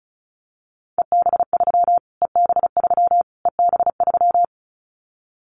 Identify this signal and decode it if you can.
Morse code, heard as a keyed tone, about 34 wpm, 700 Hz. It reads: EB3 EB3 EB3
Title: Callsign Prefixes for Top 25 DXCC - Repeated 3x - 35wpm